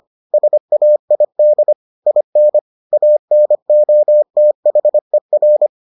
Morse code, heard as a keyed tone, about 25 wpm, 600 Hz. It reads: SAID IN ANOTHER